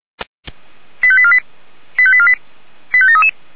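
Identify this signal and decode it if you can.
Five-tone selective call (ZVEI), heard as quick runs of five tones, tones of about 70 ms. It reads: 85848 85848 85733
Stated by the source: ZVEI five-tone selective call received with an SDR.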